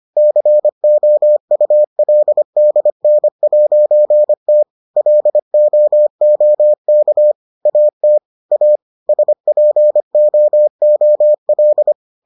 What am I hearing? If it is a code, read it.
Morse code, heard as a keyed tone, about 25 wpm, 600 Hz. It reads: COULDN'T LOOK AT A SPOOL